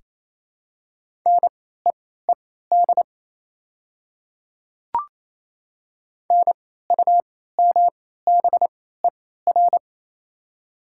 Morse code, heard as a keyed tone, about 28 wpm, spaced as for 16 wpm, 700 Hz. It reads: NEED NUMBER
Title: 3x Character Spacing - Top 200 Words - 28wpm